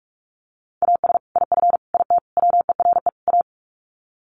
Morse code, heard as a keyed tone, about 44 wpm, 700 Hz. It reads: AS IF IT WERE A
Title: Sets of 5 Words 45wpm